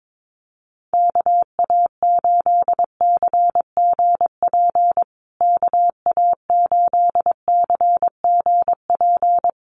Morse code, heard as a keyed tone, about 22 wpm, 700 Hz. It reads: KA8CGP KA8CGP
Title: USA Call Signs Repeated 2x 22wpm